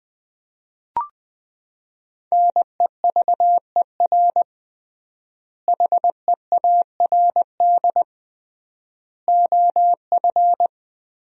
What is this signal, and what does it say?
Morse code, heard as a keyed tone, about 19 wpm, 700 Hz. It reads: NEVER HEARD OF
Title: 3x Word-Spacing - Sentences from Top 400 Words - 20wpm